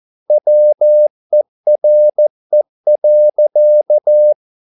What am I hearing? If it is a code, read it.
Morse code, heard as a keyed tone, about 14 wpm, 600 Hz. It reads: WERE.